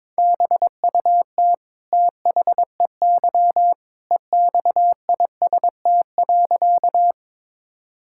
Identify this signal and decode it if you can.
Morse code, heard as a keyed tone, about 21 wpm, 700 Hz. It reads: BUT THEY EXIST.
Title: Sentences from Top 1000 Words - Review Part 5 - 22wpm